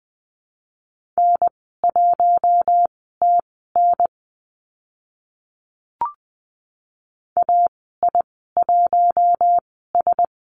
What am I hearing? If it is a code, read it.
Morse code, heard as a keyed tone, about 20 wpm, spaced as for 14 wpm, 700 Hz. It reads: N1TN AI1S
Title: 042 Lesson 8g - Number 1 - Call Signs 20wpm